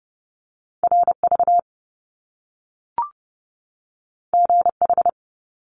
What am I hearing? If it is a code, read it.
Morse code, heard as a keyed tone, about 30 wpm, 700 Hz. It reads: RV GH
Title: Top Two Letter Combinations in Words 30wpm